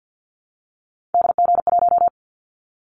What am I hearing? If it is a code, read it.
Morse code, heard as a keyed tone, about 50 wpm, 700 Hz. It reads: DG1
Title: Callsign Prefixes for Top 25 DXCC - 50wpm